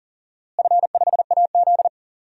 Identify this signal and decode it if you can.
Morse code, heard as a keyed tone, about 40 wpm, 700 Hz. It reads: F5AZ
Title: DXCC-100 Call Signs - 4 Characters - Repeated 2x - 40wpm